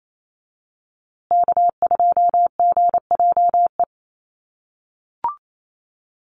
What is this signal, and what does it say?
Morse code, heard as a keyed tone, about 28 wpm, 700 Hz. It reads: K2GJE
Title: USA Call Signs Repeated 2x 28wpm